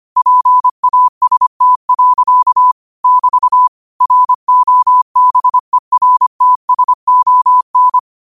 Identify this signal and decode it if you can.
Morse code, heard as a keyed tone, about 25 wpm, 1000 Hz. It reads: PAST. <BT> ROBERTSON
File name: qotdfiles / 25wpm